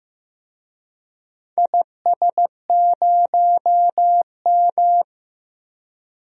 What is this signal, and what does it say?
Morse code, heard as a keyed tone, about 15 wpm, 700 Hz. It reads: IS0M
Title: DXCC-100 Call Signs - 4 Characters - 15wpm